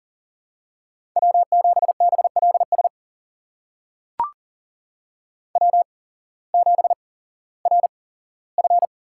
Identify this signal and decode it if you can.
Morse code, heard as a keyed tone, about 42 wpm, spaced as for 19 wpm, 700 Hz. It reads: W7BLS W7RF
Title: USA Call Signs - 8x Character Spacing - 40wpm